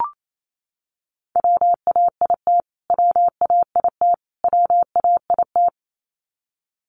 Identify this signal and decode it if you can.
Morse code, heard as a keyed tone, about 28 wpm, 700 Hz. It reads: WAIT WAIT WAIT
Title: Top 500 Words Repeated 28wpm